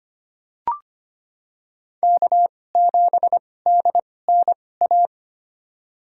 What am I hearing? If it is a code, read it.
Morse code, heard as a keyed tone, about 25 wpm, spaced as for 18 wpm, 700 Hz. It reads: K7DNA
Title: USA Call Signs - 5 Characters - 2x Character Spacing - 25wpm